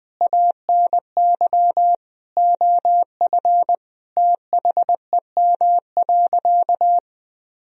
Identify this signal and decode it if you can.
Morse code, heard as a keyed tone, about 20 wpm, 700 Hz. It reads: ANY OF THEM.
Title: Sentences from Top 1000 Words - Review Part 1 - 20wpm